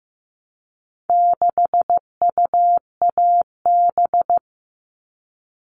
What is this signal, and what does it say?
Morse code, heard as a keyed tone, about 15 wpm, 700 Hz. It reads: 6UAB
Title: All Cut Numbers - 4-Digit 15wpm